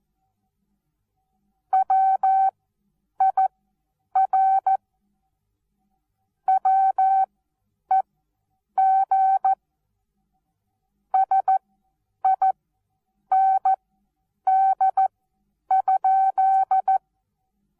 Morse code, heard as a keyed tone, about 14 wpm, spaced as for 8 wpm, 770 Hz. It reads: WIR WEG SIND?
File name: Onlinelehrgang / morsen / DJ4UF Morsen19